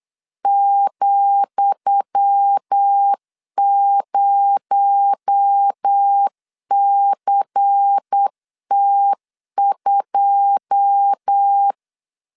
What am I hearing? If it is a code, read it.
Morse code, heard as a keyed tone, about 8 wpm, 800 Hz. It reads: ,0CT2
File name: LEZIONI / 10 SIGNS / 10D COMA 8wpm 3749